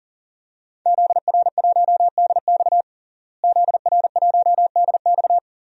Morse code, heard as a keyed tone, about 40 wpm, 700 Hz. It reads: ZR1DX ZR1DX
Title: DXCC-100 Call Signs - 5 Characters - Repeated 3x - 40wpm